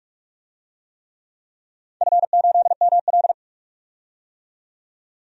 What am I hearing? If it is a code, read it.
Morse code, heard as a keyed tone, about 44 wpm, 700 Hz. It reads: F8ML
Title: DXCC-100 Call Signs - 4 Characters - 45wpm